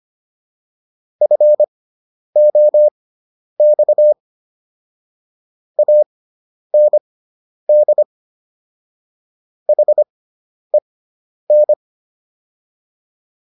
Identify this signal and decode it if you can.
Morse code, heard as a keyed tone, about 25 wpm, spaced as for 10 wpm, 600 Hz. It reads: FOX AND HEN